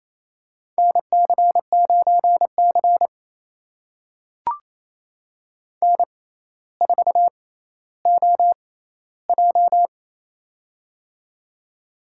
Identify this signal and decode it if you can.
Morse code, heard as a keyed tone, about 28 wpm, spaced as for 14 wpm, 700 Hz. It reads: NC9C N4OJ
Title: USA Call Signs - 4 Characters - 6x Character Spacing - 28wpm